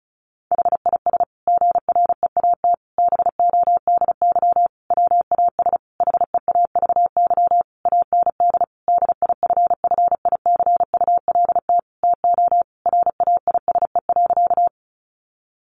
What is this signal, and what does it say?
Morse code, heard as a keyed tone, about 34 wpm, 700 Hz. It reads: HIS GREAT BODY WAS HEAVY AND DIFFICULT TO RAISE.
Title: Sentences from Top 1400 Words 35wpm